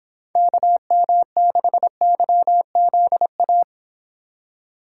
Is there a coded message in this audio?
Yes: Morse code from 0.4 to 3.6 s, heard as a keyed tone, about 26 wpm, 700 Hz. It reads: KM6YZA